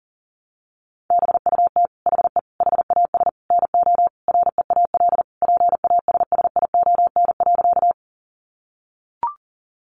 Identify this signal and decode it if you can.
Morse code, heard as a keyed tone, about 40 wpm, 700 Hz. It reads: BUT HE HAS NO REAL PASSION.
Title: Sentences from Top 1300 Words 40wpm